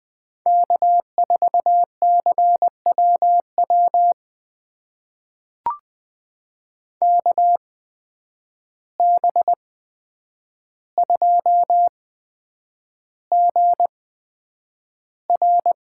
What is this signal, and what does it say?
Morse code, heard as a keyed tone, about 20 wpm, spaced as for 9 wpm, 700 Hz. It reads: K4CWW KB2GR
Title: USA Call Signs - 8x Character Spacing - 20wpm